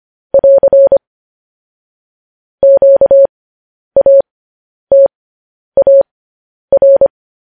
Morse code, heard as a keyed tone, about 25 wpm, spaced as for 10 wpm, 550 Hz. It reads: <AR> QATAR